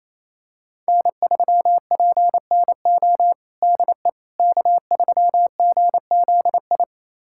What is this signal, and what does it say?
Morse code, heard as a keyed tone, about 27 wpm, 700 Hz. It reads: N3PNO DE K3GZI
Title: USA Callsign DE Callsign - 5 Characters - 28wpm